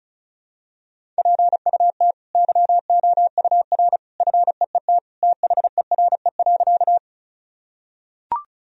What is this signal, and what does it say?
Morse code, heard as a keyed tone, about 36 wpm, 700 Hz. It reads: PUT YOUR FEET THERE.
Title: Sentences from Top 1000 Words - Review Part 1 - 35wpm